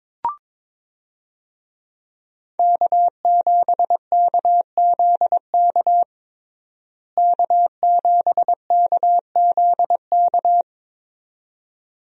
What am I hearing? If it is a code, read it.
Morse code, heard as a keyed tone, about 22 wpm, 700 Hz. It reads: K7KZK K7KZK
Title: USA Call Signs - 5 Characters - Repeated 2x - 22wpm